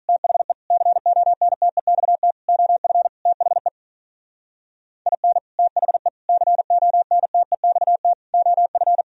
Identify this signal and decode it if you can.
Morse code, heard as a keyed tone, about 44 wpm, 700 Hz. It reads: THE CONTEXT OF THE IN THE CONTEXT OF